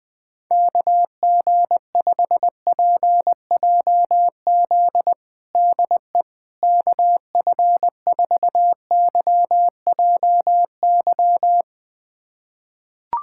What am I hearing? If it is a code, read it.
Morse code, heard as a keyed tone, about 20 wpm, 700 Hz. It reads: KG5PJZ DE KF4YJY
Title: USA Callsign DE Callsign - 6 Characters - 20wpm